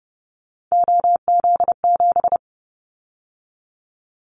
Morse code, heard as a keyed tone, about 30 wpm, 700 Hz. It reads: OZ7
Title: Callsign Prefixes for Top 100 DXCC - 30wpm